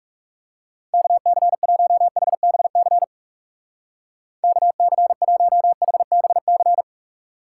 Morse code, heard as a keyed tone, about 34 wpm, 700 Hz. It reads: KC1HBC KC1HBC
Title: Speed Racing - USA 2x3 Call Signs 30wpm